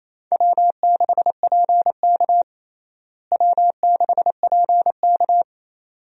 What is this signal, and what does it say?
Morse code, heard as a keyed tone, about 28 wpm, 700 Hz. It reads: W6PK W6PK
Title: USA Call Signs - 4 Characters - Repeated 2x - 28wpm